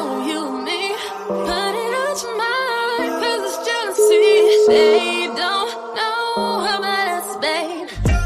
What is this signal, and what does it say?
Ringback tone (Europe, 425 Hz), with noise.